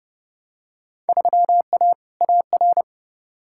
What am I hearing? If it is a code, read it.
Morse code, heard as a keyed tone, about 30 wpm, 700 Hz. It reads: 3A AR
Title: Field Day Exchanges Encore - 30wpm